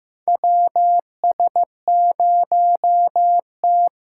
Morse code, heard as a keyed tone, about 15 wpm, 700 Hz. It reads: W S 0 T
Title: USA Call Signs - 4 Characters - 6x Character Spacing - 15wpm